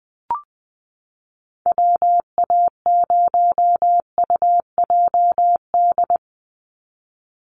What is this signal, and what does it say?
Morse code, heard as a keyed tone, about 20 wpm, 700 Hz. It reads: WA0UJD
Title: USA Call Signs 20wpm